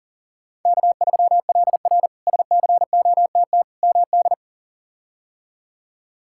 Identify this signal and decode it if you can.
Morse code, heard as a keyed tone, about 40 wpm, 700 Hz. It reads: K3LR SCOTT MD
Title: K1USN SST QSO Practice - 40wpm